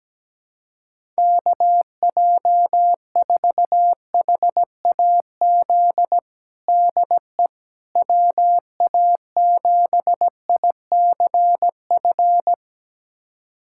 Morse code, heard as a keyed tone, about 17 wpm, 700 Hz. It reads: KJ4HAZ DE WA7ICF
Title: USA Callsign DE Callsign - 6 Characters - 17wpm